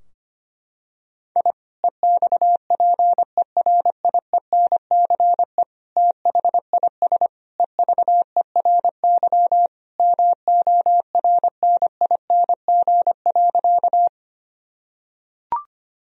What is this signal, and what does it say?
Morse code, heard as a keyed tone, about 25 wpm, 700 Hz. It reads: I EXPERIENCE THIS EVERY MORNING.
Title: Sentences from Top 1000 Words - Review Part 3 - 25wpm